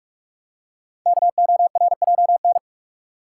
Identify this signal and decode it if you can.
Morse code, heard as a keyed tone, about 44 wpm, 700 Hz. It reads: KORJN
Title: Code Groups - 5 Letters - 45wpm